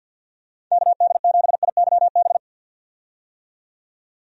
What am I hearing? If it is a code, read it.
Morse code, heard as a keyed tone, about 50 wpm, 700 Hz. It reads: KD7IYB
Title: USA Call Signs - 6 Characters - 50wpm